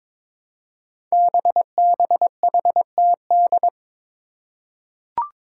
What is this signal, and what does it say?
Morse code, heard as a keyed tone, about 22 wpm, 700 Hz. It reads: BBHTD
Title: Five Challenging Letter-Number Combinations 22wpm